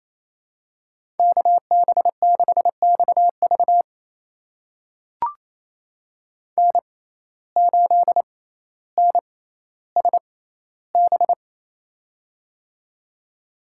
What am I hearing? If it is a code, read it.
Morse code, heard as a keyed tone, about 28 wpm, spaced as for 14 wpm, 700 Hz. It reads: KB6XV N8NSB
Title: USA Call Signs - 5 Characters - 6x Character Spacing - 28wpm